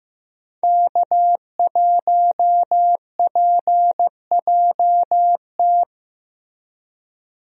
Morse code, heard as a keyed tone, about 16 wpm, 700 Hz. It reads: K1PJT
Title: USA Call Signs - 5 Characters - Repeated 3x - 15wpm